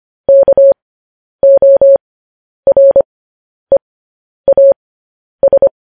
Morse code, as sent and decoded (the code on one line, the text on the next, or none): -.- --- .-. .- ...
KOREAS